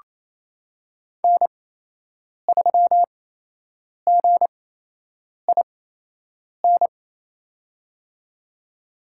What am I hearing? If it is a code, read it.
Morse code, heard as a keyed tone, about 28 wpm, spaced as for 8 wpm, 700 Hz. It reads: N3GIN
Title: USA Call Signs - 5 Characters - 8x Character Spacing - 28wpm